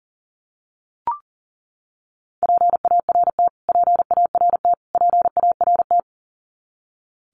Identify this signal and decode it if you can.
Morse code, heard as a keyed tone, about 40 wpm, 700 Hz. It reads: PART PART PART
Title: Top 500 Words Repeated 40wpm